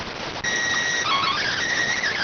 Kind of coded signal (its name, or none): SSTV (slow-scan television)